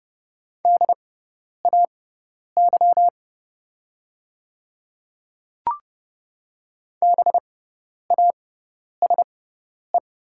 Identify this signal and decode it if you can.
Morse code, heard as a keyed tone, about 30 wpm, spaced as for 10 wpm, 700 Hz. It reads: DAY BASE
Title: 6x Character Spacing - Top 300 Words - 30wpm